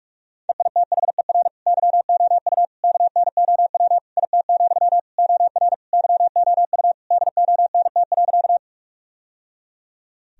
Morse code, heard as a keyed tone, about 45 wpm, 700 Hz. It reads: EITHER YOU KNOW IT, OR YOU DONT.
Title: Sentences from Top 1000 Words - Review Part 4 - 45wpm